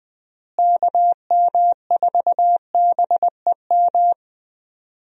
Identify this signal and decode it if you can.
Morse code, heard as a keyed tone, about 20 wpm, 700 Hz. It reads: KM4BEM
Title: USA Call Signs - 6 Characters - Repeated 2x - 20wpm